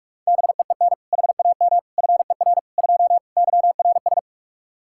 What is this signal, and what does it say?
Morse code, heard as a keyed tone, about 46 wpm, 700 Hz. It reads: BEEN HAM FER 2 YRS
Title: Practice QSO - Exchange 3 - More Info - 45wpm